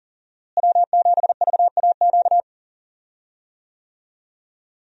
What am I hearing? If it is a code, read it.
Morse code, heard as a keyed tone, about 40 wpm, 700 Hz. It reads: W7VAQ
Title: USA Call Signs - 5 Characters - Repeated 2x - 40wpm